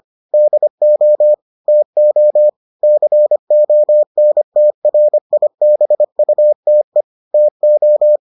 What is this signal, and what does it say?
Morse code, heard as a keyed tone, about 25 wpm, 600 Hz. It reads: DO TO CONTRIBUTE TO